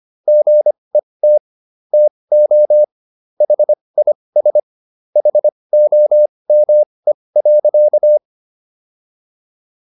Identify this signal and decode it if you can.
Morse code, heard as a keyed tone, about 25 wpm, spaced as for 20 wpm, 600 Hz. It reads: GET TO HIS HOME.